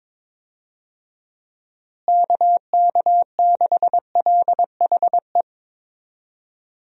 Morse code, heard as a keyed tone, about 22 wpm, 700 Hz. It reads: KK6LHE
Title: USA Call Signs - 6 Characters - Repeated 2x - 22wpm